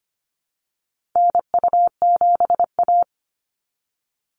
Morse code, heard as a keyed tone, about 25 wpm, 700 Hz. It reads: NU7A
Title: NA SOTA Chaser Callsigns 25wpm